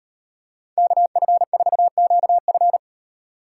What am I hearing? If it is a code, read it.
Morse code, heard as a keyed tone, about 38 wpm, 700 Hz. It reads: KF4QF